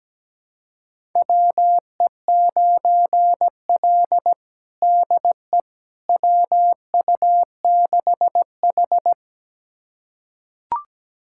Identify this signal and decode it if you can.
Morse code, heard as a keyed tone, about 17 wpm, 700 Hz. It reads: WE9L DE WU6H
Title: USA Callsign DE Callsign - 4 Characters - 17wpm